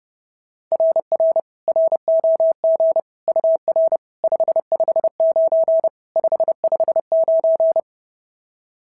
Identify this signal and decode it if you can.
Morse code, heard as a keyed tone, about 30 wpm, 690 Hz. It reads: RR ROG UR 559 559
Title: SOTA QSO Practice 30wpm